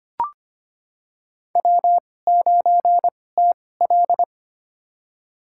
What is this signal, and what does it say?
Morse code, heard as a keyed tone, about 25 wpm, spaced as for 18 wpm, 700 Hz. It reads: W9TL
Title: USA Call Signs - 4 Characters - 2x Character Spacing - 25wpm